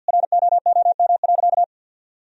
Morse code, heard as a keyed tone, about 49 wpm, 700 Hz. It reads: ROOM.
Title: Sentences from Top 1000 Words - Review Part 1 - 50wpm